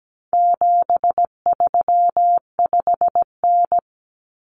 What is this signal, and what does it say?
Morse code, heard as a keyed tone, about 17 wpm, 700 Hz. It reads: 735N
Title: Common Cut Numbers - 4-Digit 17wpm